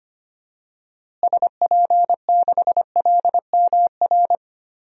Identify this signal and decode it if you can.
Morse code, heard as a keyed tone, about 25 wpm, 700 Hz. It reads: SP6LMR
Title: DXCC-100 Call Signs - 6 Characters - Repeated 2x - 25wpm